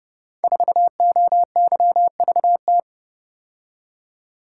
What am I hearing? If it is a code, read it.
Morse code, heard as a keyed tone, about 29 wpm, 700 Hz. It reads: 4OYVT
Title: Code Groups - 5 Letter:Numbers - 30wpm